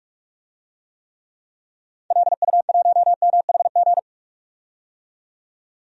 Morse code, heard as a keyed tone, about 47 wpm, 700 Hz. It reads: LU1MHG
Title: DXCC-100 Call Signs - 6 Characters - 45wpm